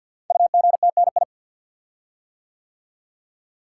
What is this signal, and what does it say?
Morse code, heard as a keyed tone, about 50 wpm, 700 Hz. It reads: UGTNI
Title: Code Groups - 5 Letters - 50wpm